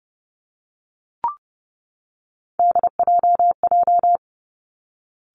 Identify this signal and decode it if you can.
Morse code, heard as a keyed tone, about 30 wpm, 700 Hz. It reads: DJJ
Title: Three Challenging Letter-Number Combinations 30wpm